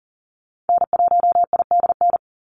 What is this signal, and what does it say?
Morse code, heard as a keyed tone, about 40 wpm, 700 Hz. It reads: N1IDN
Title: NA SOTA Activator Callsigns 40wpm